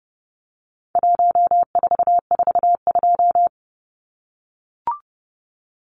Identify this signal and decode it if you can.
Morse code, heard as a keyed tone, about 30 wpm, 700 Hz. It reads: 1442
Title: Numbers - 4-Digit 30wpm